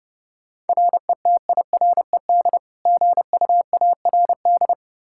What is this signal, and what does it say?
Morse code, heard as a keyed tone, about 30 wpm, 700 Hz. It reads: RETIRED GUARD
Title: K1USN SST QSO Practice 30wpm